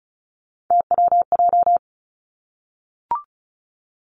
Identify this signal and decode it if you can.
Morse code, heard as a keyed tone, about 36 wpm, 700 Hz. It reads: TWJ
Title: Three Challenging Letter-Number Combinations 35wpm